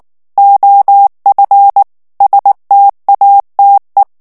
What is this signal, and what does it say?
Morse code, heard as a keyed tone, about 19 wpm, 800 Hz. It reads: OF STATE
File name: ANGLAIS texte clair 3 vitesse 18 mots minute